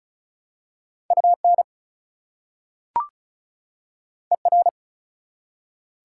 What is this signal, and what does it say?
Morse code, heard as a keyed tone, about 34 wpm, 700 Hz. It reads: UN ER
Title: Prefixes and Suffixes 35wpm